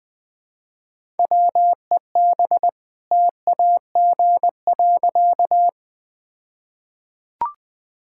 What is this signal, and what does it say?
Morse code, heard as a keyed tone, about 20 wpm, 700 Hz. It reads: WEB TAG.
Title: All-Age Appropriate Jokes - 20wpm